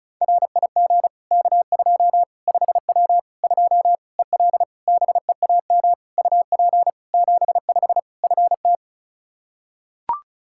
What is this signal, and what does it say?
Morse code, heard as a keyed tone, about 36 wpm, 700 Hz. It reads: RIG K2 5W 2 EL BEAM UP 75 FT
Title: Practice QSO - Exchange 2 - Info - 35wpm